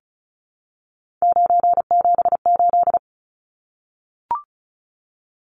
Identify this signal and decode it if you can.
Morse code, heard as a keyed tone, about 34 wpm, 700 Hz. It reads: 978
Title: Numbers - 3-Digit 35wpm